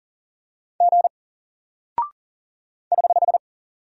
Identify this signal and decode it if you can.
Morse code, heard as a keyed tone, about 40 wpm, 700 Hz. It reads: G <HH>
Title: Single Character (Letter:Number:Punctuation:Prosign) Alt 40wpm